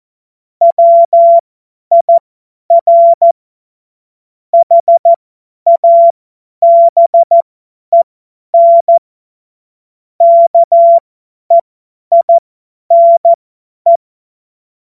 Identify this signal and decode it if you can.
Morse code, heard as a keyed tone, about 14 wpm, spaced as for 10 wpm, 680 Hz. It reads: WIR HABEN KEINE
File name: Morsekurs / Witze / Witz 02. Dezember 2025 50 F70